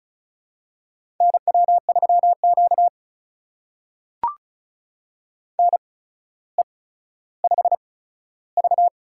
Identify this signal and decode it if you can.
Morse code, heard as a keyed tone, about 36 wpm, spaced as for 15 wpm, 700 Hz. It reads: NW3Q NE5V
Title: USA Call Signs - 4 Characters - 8x Character Spacing - 35wpm